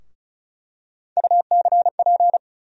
Morse code, heard as a keyed tone, about 36 wpm, 700 Hz. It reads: UCP
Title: USA Call Sign Postfixes - 3 Letters 35wpm